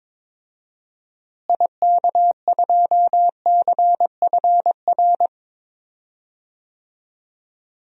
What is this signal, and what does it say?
Morse code, heard as a keyed tone, about 22 wpm, 700 Hz. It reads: IK2CFR
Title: DXCC-100 Call Signs - 6 Characters - Repeated 2x - 22wpm